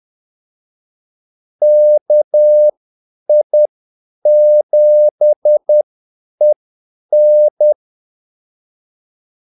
Morse code, heard as a keyed tone, about 10 wpm, spaced as for 8 wpm, 600 Hz. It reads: KI7EN
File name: Callsigns-10-8wpm